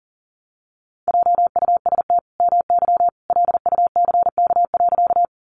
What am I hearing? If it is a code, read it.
Morse code, heard as a keyed tone, about 40 wpm, 700 Hz. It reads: JUST MY LUCK.
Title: Sentences from Top 1500 Words 40wpm